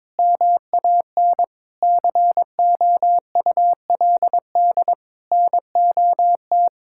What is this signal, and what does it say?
Morse code, heard as a keyed tone, about 22 wpm, 700 Hz. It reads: MAN COULD NOT
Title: Sentences from Top 1600 Words 22wpm